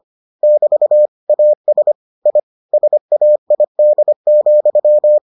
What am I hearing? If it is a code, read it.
Morse code, heard as a keyed tone, about 25 wpm, 600 Hz. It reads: <BT> AS I SAID,